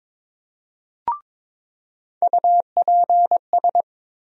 Morse code, heard as a keyed tone, about 22 wpm, 700 Hz. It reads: UPS
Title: USA Call Sign Postfixes - 3 Letters 22wpm